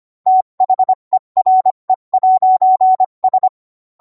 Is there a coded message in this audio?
Yes: Morse code from 0.3 to 3.5 s, heard as a keyed tone, about 25 wpm, 750 Hz. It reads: THERE'S